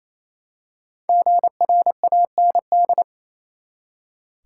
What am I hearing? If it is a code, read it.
Morse code, heard as a keyed tone, about 28 wpm, 700 Hz. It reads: GRAND
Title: Five Letter Words - 28wpm